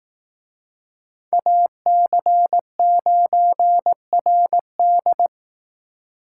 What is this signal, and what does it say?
Morse code, heard as a keyed tone, about 18 wpm, 700 Hz. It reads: AC9RD
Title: Speed Racing - USA Call Signs 15wpm